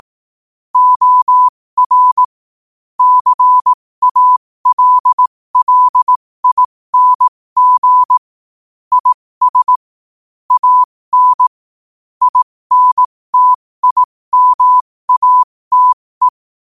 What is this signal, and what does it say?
Morse code, heard as a keyed tone, about 18 wpm, spaced as for 15 wpm, 1000 Hz. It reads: OR CALLING IS AN INTIMATE
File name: qotdfiles / 15wpm